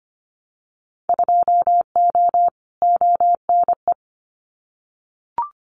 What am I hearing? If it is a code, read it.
Morse code, heard as a keyed tone, about 25 wpm, 700 Hz. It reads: 2O ONE
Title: Winter Field Day Exchanges Encore 25wpm